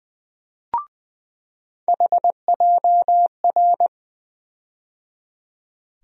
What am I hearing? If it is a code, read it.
Morse code, heard as a keyed tone, about 20 wpm, 700 Hz. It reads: HJR
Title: USA Call Sign Postfixes - 3 Letters 20wpm